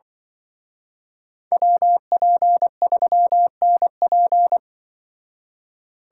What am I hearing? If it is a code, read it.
Morse code, heard as a keyed tone, about 24 wpm, 700 Hz. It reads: WP3NP
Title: Speed Racing - USA 2x2 Call Signs 20wpm